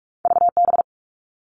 Morse code, heard as a keyed tone, about 46 wpm, 700 Hz. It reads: VB
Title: All Cut Numbers - 2-Digit 45wpm